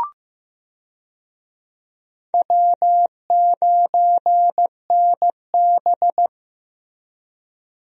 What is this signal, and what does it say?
Morse code, heard as a keyed tone, about 15 wpm, 700 Hz. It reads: W9NB
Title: USA Call Signs - 4 Characters - Repeated 2x - 15wpm